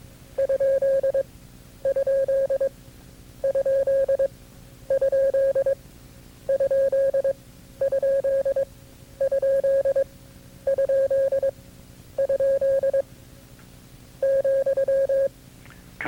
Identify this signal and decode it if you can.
Morse code, heard as a keyed tone, about 22 wpm, spaced as for 10 wpm, 540 Hz. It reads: ????????? ,
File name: CW-practice/KB6MT-tape2-side1